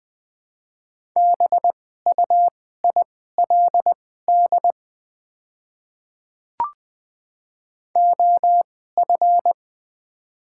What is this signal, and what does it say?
Morse code, heard as a keyed tone, about 21 wpm, spaced as for 15 wpm, 700 Hz. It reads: BUILD OF
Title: 2x Character Spacing - Top 300 Words - 20wpm